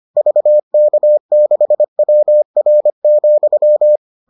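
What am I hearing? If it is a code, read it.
Morse code, heard as a keyed tone, about 25 wpm, 600 Hz. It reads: VK6WR,
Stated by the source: Morse edition of the Foundations of Amateur Radio episodes.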